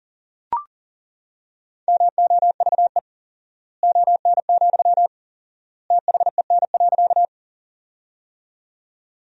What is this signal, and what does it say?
Morse code, heard as a keyed tone, about 40 wpm, 700 Hz. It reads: MOVE ON, THEN.
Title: 4x Word-Spacing - Sentences from Top 200 Words - 40wpm